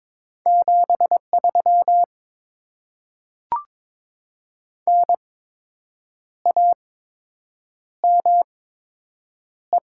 Morse code, heard as a keyed tone, about 22 wpm, spaced as for 7 wpm, 700 Hz. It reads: 73 NAME
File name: Top 90 QSO Elements - 8x Character Spacing - 22wpm